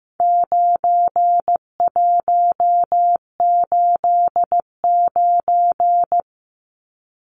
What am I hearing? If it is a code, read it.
Morse code, heard as a keyed tone, about 15 wpm, 700 Hz. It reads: 9189
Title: Numbers - 4-Digit 15wpm